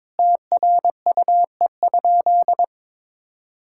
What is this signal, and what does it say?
Morse code, heard as a keyed tone, about 21 wpm, 700 Hz. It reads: TRUE?
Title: Sentences from Top 1000 Words - Review Part 6 - 22wpm